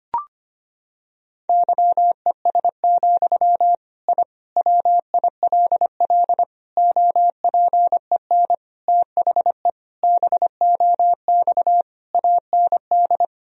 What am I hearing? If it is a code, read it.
Morse code, heard as a keyed tone, about 25 wpm, 700 Hz. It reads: YES, I WILL OPEN THE BOX AND
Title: Sentences from Top 1000 Words - Review Part 2 - 25wpm